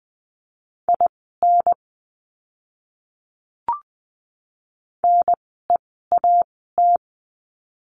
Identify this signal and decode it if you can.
Morse code, heard as a keyed tone, about 20 wpm, spaced as for 14 wpm, 700 Hz. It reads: IN NEAT
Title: 030 Lesson 6f - Letter i - Words 20wpm